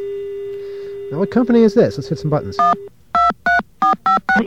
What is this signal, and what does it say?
DTMF, heard as telephone touch tones, tones of about 130 ms, with speech. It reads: 123456